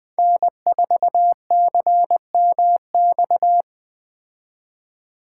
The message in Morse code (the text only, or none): N4CMX